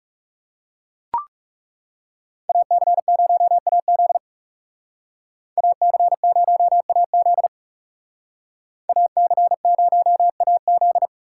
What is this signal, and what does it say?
Morse code, heard as a keyed tone, about 35 wpm, 700 Hz. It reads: AC0AZ AC0AZ AC0AZ
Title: Speed Racing - USA 2x2 Call Signs 30wpm